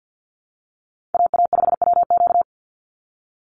Morse code, heard as a keyed tone, about 50 wpm, 700 Hz. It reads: AA5WQ
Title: USA Call Signs 50wpm